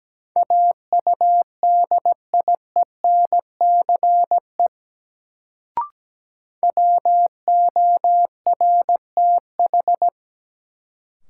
Morse code, heard as a keyed tone, about 17 wpm, 700 Hz. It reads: AUDIENCE WORTH
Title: Sentences from Top 800 Words - New Word Drill - 17wpm